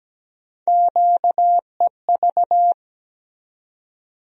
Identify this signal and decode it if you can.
Morse code, heard as a keyed tone, about 17 wpm, 700 Hz. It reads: QEV
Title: USA Call Sign Postfixes - 3 Letters 17wpm